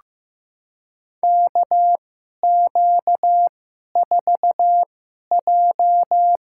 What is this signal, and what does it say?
Morse code, heard as a keyed tone, about 15 wpm, spaced as for 11 wpm, 700 Hz. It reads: KQ4J
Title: USA Call Signs - 2x Character Spacing - 15wpm